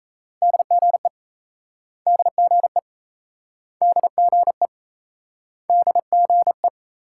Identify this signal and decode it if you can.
Morse code, heard as a keyed tone, about 38 wpm, 700 Hz. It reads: DGE DGE DGE DGE